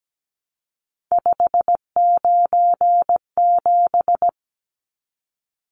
Morse code, heard as a keyed tone, about 17 wpm, 700 Hz. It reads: 597
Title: Numbers - 3-Digit 17wpm